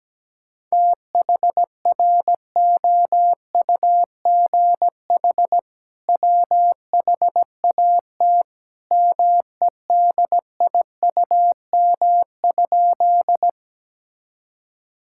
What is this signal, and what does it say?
Morse code, heard as a keyed tone, about 17 wpm, 700 Hz. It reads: THROUGH WHAT MEDIUM?
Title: Sentences from Top 1000 Words - Review Part 6 - 17wpm